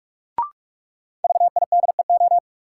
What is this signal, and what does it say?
Morse code, heard as a keyed tone, about 45 wpm, 700 Hz. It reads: VIDEO
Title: Sentences from Top 1000 Words - Review Part 4 - New Word Drill - 45wpm